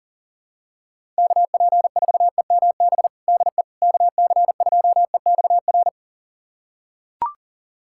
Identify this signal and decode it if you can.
Morse code, heard as a keyed tone, about 40 wpm, 700 Hz. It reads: KP4EMB DE KC2EXR
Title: USA Callsign DE Callsign 40wpm